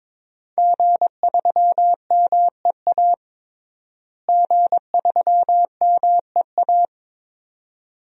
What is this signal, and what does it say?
Morse code, heard as a keyed tone, about 22 wpm, 700 Hz. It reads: G3MEA G3MEA
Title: DXCC-100 Call Signs - 5 Characters - Repeated 2x - 22wpm